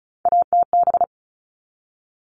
Morse code, heard as a keyed tone, about 34 wpm, 700 Hz. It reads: ATB
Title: All Cut Numbers - 3-Digit 35wpm